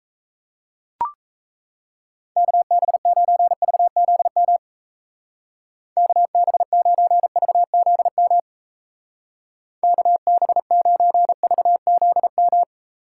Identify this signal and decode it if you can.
Morse code, heard as a keyed tone, about 37 wpm, 700 Hz. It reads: KB9VZM KB9VZM KB9VZM